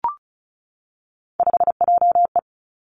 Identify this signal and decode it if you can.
Morse code, heard as a keyed tone, about 36 wpm, 700 Hz. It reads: 5JE
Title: Three Challenging Letter-Number Combinations 35wpm